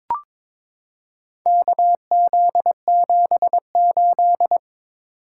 Morse code, heard as a keyed tone, about 22 wpm, 700 Hz. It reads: KZ78
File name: practice / 22wpm / Four Letter-Number Combinations - 22wpm